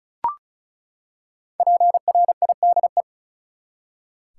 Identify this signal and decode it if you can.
Morse code, heard as a keyed tone, about 34 wpm, 700 Hz. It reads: PRIDE
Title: Five Letter Words - 35wpm